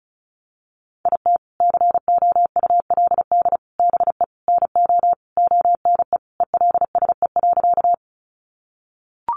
Morse code, heard as a keyed tone, about 34 wpm, 700 Hz. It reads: IT COULD BE NO ONE ELSE.
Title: Sentences from Top 800 Words 35wpm